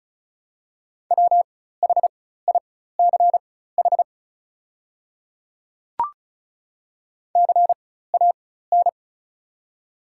Morse code, heard as a keyed tone, about 36 wpm, spaced as for 16 wpm, 700 Hz. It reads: WHICH CAN
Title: 4x Character Spacing - Top 200 Words - 35wpm